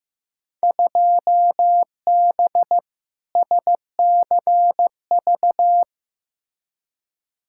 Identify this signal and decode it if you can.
Morse code, heard as a keyed tone, about 15 wpm, 700 Hz. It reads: 2B SCV
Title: Field Day Exchanges Encore - 15wpm